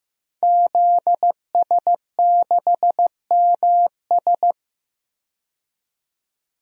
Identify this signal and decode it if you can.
Morse code, heard as a keyed tone, about 15 wpm, 700 Hz. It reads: ZS6MS
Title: DXCC-100 Call Signs - 5 Characters - 15wpm